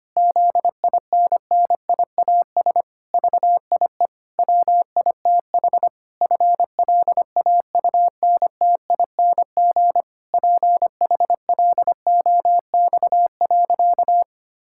Morse code, heard as a keyed tone, about 25 wpm, 700 Hz. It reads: ZINNIAS VIE WITH FLAUNTING PHLOX.
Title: Pangram Challenge v2 - 25wpm